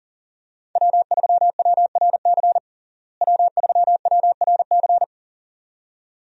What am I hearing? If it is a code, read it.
Morse code, heard as a keyed tone, about 37 wpm, 700 Hz. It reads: W3WRC W3WRC
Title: USA Call Signs - 5 Characters - Repeated 2x - 40wpm